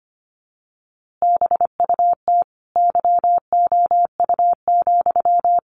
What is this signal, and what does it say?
Morse code, heard as a keyed tone, about 25 wpm, 700 Hz. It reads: BUT YOU,
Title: A Sound Of Thunder 25wpm